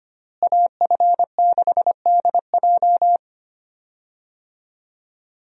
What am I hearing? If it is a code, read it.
Morse code, heard as a keyed tone, about 25 wpm, 700 Hz. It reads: AF6DJ
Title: USA Call Signs - 5 Characters - Repeated 3x - 25wpm